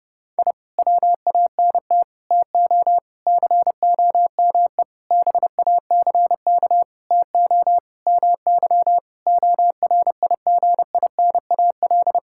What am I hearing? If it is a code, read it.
Morse code, heard as a keyed tone, about 30 wpm, 700 Hz. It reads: I WANT TO COME BACK TO MY ORIGINAL
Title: Sentences from Top 1000 Words - Review Part 6 - 30wpm